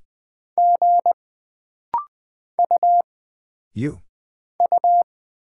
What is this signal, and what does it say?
Morse code, heard as a keyed tone, about 20 wpm, 700 Hz. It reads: G U U
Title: Single Character (Letter:Number:Punctuation:Prosign) - Rapid-Fire - 20wpm